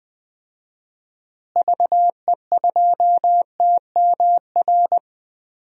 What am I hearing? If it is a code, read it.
Morse code, heard as a keyed tone, about 20 wpm, 700 Hz. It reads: VE2TMR
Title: DXCC-100 Call Signs - 6 Characters - Repeated 3x - 20wpm